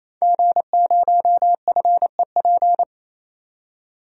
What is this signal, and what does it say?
Morse code, heard as a keyed tone, about 28 wpm, 700 Hz. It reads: G0FEP
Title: DXCC-100 Call Signs - 5 Characters - 28wpm